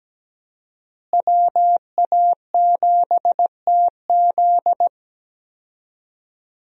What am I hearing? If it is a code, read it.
Morse code, heard as a keyed tone, about 17 wpm, 700 Hz. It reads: WA7TZ
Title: USA Call Signs - 5 Characters - Repeated 2x - 17wpm